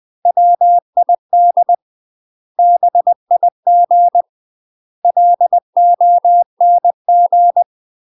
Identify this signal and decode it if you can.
Morse code, heard as a keyed tone, about 20 wpm, 700 Hz. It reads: WID BIG LONG
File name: UK MASTER QSOs TRAINING / 20WPM / UK MASTER QSOs 0058